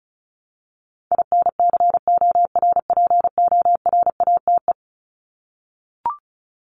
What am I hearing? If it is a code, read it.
Morse code, heard as a keyed tone, about 34 wpm, 700 Hz. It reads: INCORPORATE
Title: Top 1751-2000 Words 35wpm